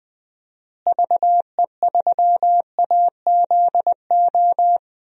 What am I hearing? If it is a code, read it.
Morse code, heard as a keyed tone, about 20 wpm, 700 Hz. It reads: VE3AZO
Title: DXCC-100 Call Signs - 6 Characters - Repeated 2x - 20wpm